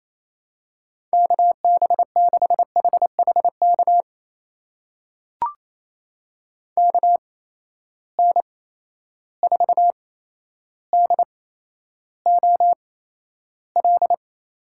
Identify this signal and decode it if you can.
Morse code, heard as a keyed tone, about 28 wpm, spaced as for 12 wpm, 700 Hz. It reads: KB6HHK KN4DOL
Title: USA Call Signs - 8x Character Spacing - 28wpm